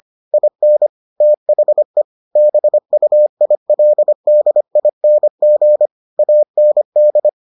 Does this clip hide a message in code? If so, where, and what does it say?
Yes: Morse code from 0.3 to 7.3 s, heard as a keyed tone, about 25 wpm, 600 Hz. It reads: IN THE BUILDING AND